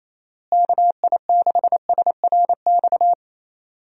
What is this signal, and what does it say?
Morse code, heard as a keyed tone, about 28 wpm, 700 Hz. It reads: KI6SRX
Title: USA Call Signs - 6 Characters - Repeated 2x - 28wpm